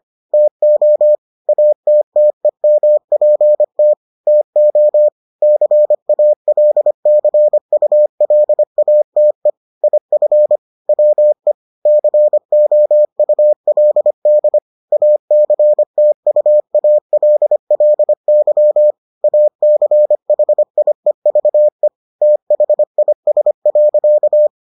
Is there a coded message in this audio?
Yes: Morse code from 0.3 to 24.5 s, heard as a keyed tone, about 25 wpm, 600 Hz. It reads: TO ATTEMPT TO CALCULATE IF WE COULD ACTUALLY ACHIEVE THIS.